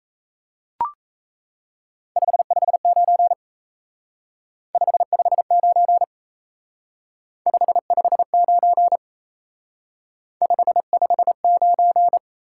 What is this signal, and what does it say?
Morse code, heard as a keyed tone, about 35 wpm, 700 Hz. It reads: 559 559 559 559
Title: Speed-Racing Top 90 QSO Elements 28wpm